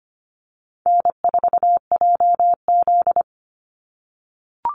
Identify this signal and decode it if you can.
Morse code, heard as a keyed tone, about 25 wpm, 700 Hz. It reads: N4JZ
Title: USA Call Signs 25wpm